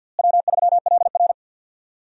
Morse code, heard as a keyed tone, about 49 wpm, 700 Hz. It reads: W3LR
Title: USA Call Signs - 4 Characters - Repeated 3x - 50wpm